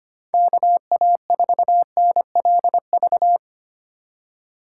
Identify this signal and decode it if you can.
Morse code, heard as a keyed tone, about 28 wpm, 700 Hz. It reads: KA4NLV
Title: Speed Racing - USA 2x3 Call Signs 25wpm